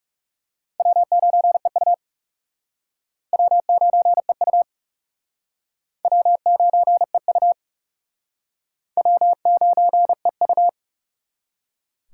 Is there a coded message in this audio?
Yes: Morse code from 0.8 to 10.7 s, heard as a keyed tone, about 40 wpm, 700 Hz. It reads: W9EU W9EU W9EU W9EU